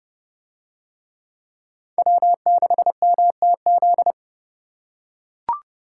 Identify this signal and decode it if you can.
Morse code, heard as a keyed tone, about 28 wpm, 700 Hz. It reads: W6MTZ